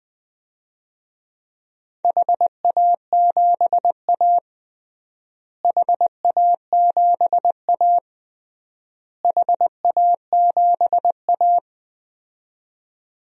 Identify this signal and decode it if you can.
Morse code, heard as a keyed tone, about 20 wpm, 700 Hz. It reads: HA7A HA7A HA7A
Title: DXCC-100 Call Signs - 4 Characters - Repeated 3x - 20wpm